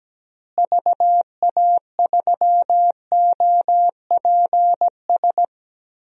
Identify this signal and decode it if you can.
Morse code, heard as a keyed tone, about 17 wpm, 700 Hz. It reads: VA3OPS
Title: DXCC-100 Call Signs - 6 Characters - Repeated 2x - 17wpm